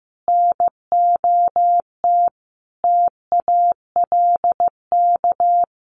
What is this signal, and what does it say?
Morse code, heard as a keyed tone, about 15 wpm, 700 Hz. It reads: NOT TALK
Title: Sentences from Top 800 Words 15wpm